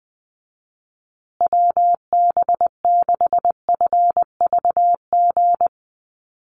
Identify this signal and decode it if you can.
Morse code, heard as a keyed tone, about 20 wpm, 700 Hz. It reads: WB6FVG
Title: USA Call Signs Repeated 3x 20wpm